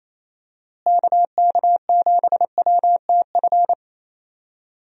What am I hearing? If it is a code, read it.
Morse code, heard as a keyed tone, about 28 wpm, 700 Hz. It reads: KK7WTF
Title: USA Call Signs - 6 Characters - Repeated 2x - 28wpm